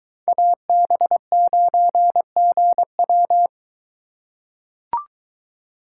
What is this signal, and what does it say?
Morse code, heard as a keyed tone, about 23 wpm, 700 Hz. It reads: AB9GW